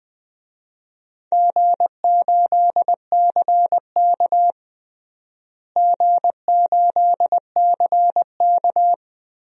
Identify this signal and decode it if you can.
Morse code, heard as a keyed tone, about 21 wpm, 700 Hz. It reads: G8CK G8CK
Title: DXCC-100 Call Signs - 4 Characters - Repeated 3x - 20wpm